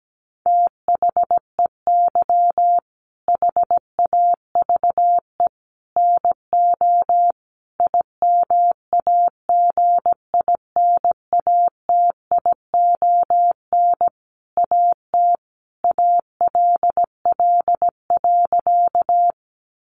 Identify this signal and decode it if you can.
Morse code, heard as a keyed tone, about 17 wpm, 700 Hz. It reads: THEY HAVE NO IMAGINATION AT ALL.
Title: Sentences from Top 1300 Words 17wpm